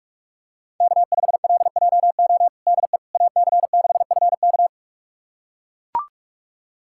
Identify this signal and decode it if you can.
Morse code, heard as a keyed tone, about 41 wpm, 700 Hz. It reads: K5LJO DE AC6FK
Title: USA Callsign DE Callsign - 5 Characters - 45wpm